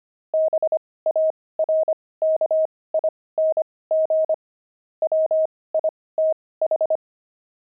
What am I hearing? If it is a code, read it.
Morse code, heard as a keyed tone, about 25 wpm, spaced as for 18 wpm, 630 Hz. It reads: BARKING WITH
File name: pangram-fs-18-cs-25-barking